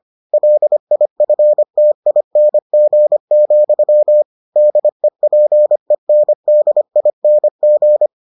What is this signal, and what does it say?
Morse code, heard as a keyed tone, about 25 wpm, 600 Hz. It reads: LIFTING, DEPENDING